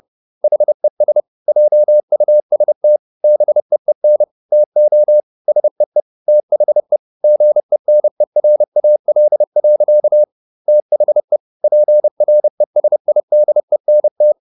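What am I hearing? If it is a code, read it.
Morse code, heard as a keyed tone, about 30 wpm, 600 Hz. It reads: HES JUST BEEN TO SEE THE GENERAL. THE PRESIDENT
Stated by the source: Morse code at 30 wpm, the novel Never by Ken Follett in English, from CW med Gnister.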